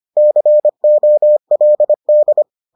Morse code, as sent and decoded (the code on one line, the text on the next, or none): -.-. --- .-.. -..
COLD